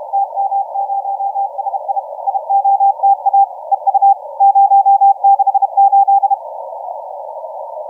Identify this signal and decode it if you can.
Morse code, heard as a keyed tone, about 31 wpm, 790 Hz, with radio noise. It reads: TM0SI IOTA EU 068
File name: TM0SI 0722UTC 17m